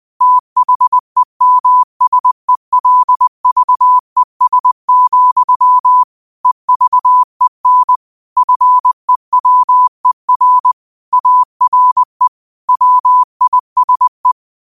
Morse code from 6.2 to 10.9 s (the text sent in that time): EVEN FEWER